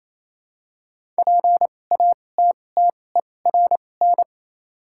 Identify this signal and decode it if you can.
Morse code, heard as a keyed tone, about 28 wpm, spaced as for 20 wpm, 690 Hz. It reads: PATTERN
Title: 2x Character Spacing - Top 500 Words - 28wpm